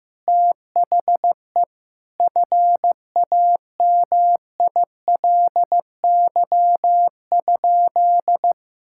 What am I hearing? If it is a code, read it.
Morse code, heard as a keyed tone, about 16 wpm, 700 Hz. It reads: THE FAMILY?
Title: Sentences from Top 1000 Words - Review Part 6 - 15wpm